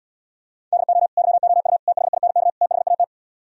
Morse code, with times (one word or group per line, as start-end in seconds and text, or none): 0.7-1.1 DO
1.2-1.8 YOU
1.9-2.5 THINK
2.6-3.0 THAT